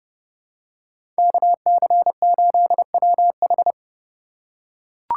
Morse code, heard as a keyed tone, about 30 wpm, 700 Hz. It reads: KC8WH
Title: K1USN SST QSO Practice - 30wpm